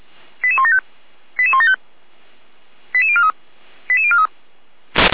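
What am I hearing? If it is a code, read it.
Five-tone selective call (ZVEI), heard as quick runs of five tones, tones of about 70 ms. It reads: 88175 88175 88043 88043
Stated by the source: ZVEI five-tone call received by SDR.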